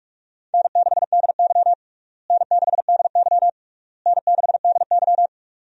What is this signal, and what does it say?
Morse code, heard as a keyed tone, about 41 wpm, 700 Hz. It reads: N6DY N6DY N6DY
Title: USA Call Signs - 4 Characters - Repeated 3x - 45wpm